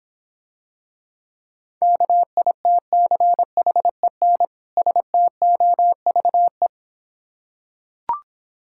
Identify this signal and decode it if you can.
Morse code, heard as a keyed tone, about 26 wpm, 700 Hz. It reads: KITCHEN STOVE